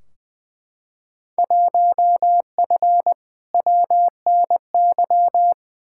Morse code, heard as a keyed tone, about 20 wpm, 700 Hz. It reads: 1F WNY
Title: Field Day Exchanges - 20wpm